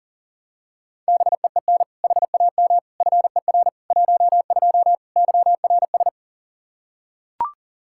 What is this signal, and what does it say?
Morse code, heard as a keyed tone, about 40 wpm, 700 Hz. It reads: BEEN HAM FER 12 YRS
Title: Practice QSO - Exchange 3 - More Info - 40wpm